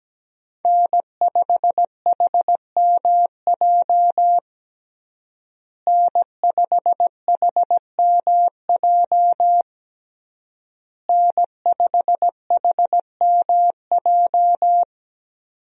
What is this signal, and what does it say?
Morse code, heard as a keyed tone, about 17 wpm, 700 Hz. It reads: N5HMJ N5HMJ N5HMJ
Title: USA Call Signs - 5 Characters - Repeated 3x - 17wpm